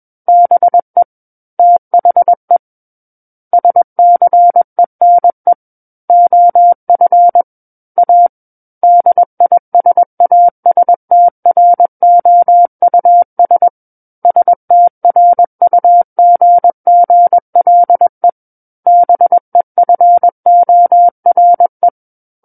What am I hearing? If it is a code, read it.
Morse code, heard as a keyed tone, about 21 wpm, 700 Hz. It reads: BE THE SCENE OF A DISASTROUS STRUGGLE BEFORE
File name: Book One / 12-Chapter 12 - 21 WPM